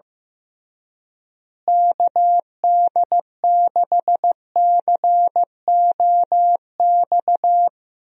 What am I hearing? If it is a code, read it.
Morse code, heard as a keyed tone, about 15 wpm, 700 Hz. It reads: KD6COX
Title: USA Call Signs - 6 Characters - Repeated 2x - 15wpm